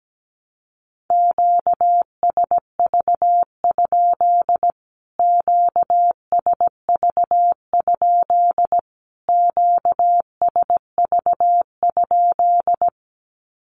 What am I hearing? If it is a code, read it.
Morse code, heard as a keyed tone, about 17 wpm, 700 Hz. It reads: QSV? QSV? QSV?
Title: Q-Codes Repeated 3x 17wpm